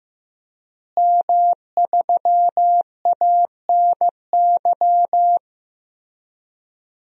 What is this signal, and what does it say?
Morse code, heard as a keyed tone, about 15 wpm, 700 Hz. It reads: M3ANY
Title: DXCC-100 Call Signs - 5 Characters - Repeated 3x - 15wpm